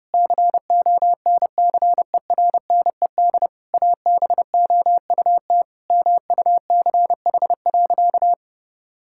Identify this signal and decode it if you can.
Morse code, heard as a keyed tone, about 30 wpm, 700 Hz. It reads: CONCERNED ABOUT MUCH.
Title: Sentences from Top 1000 Words - Review Part 6 - 30wpm